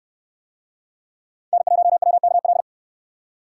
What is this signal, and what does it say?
Morse code, heard as a keyed tone, about 69 wpm, 700 Hz. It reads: N1WKL